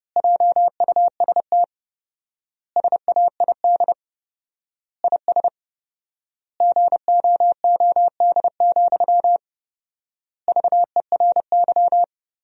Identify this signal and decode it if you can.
Morse code, heard as a keyed tone, about 30 wpm, 700 Hz. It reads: JUST SAID IS GOOD, VERY
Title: 4x Word-Spacing - Sentences from Top 200 Words - 30wpm